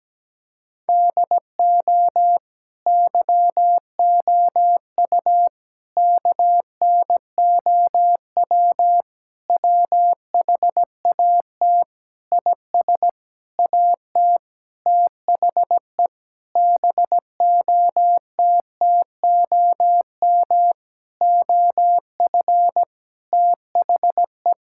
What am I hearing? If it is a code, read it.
Morse code, heard as a keyed tone, about 17 wpm, 700 Hz. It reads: DO YOU KNOW WHAT IS AT THE BOTTOM OF THE
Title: Sentences from Top 1000 Words - Review Part 6 - 17wpm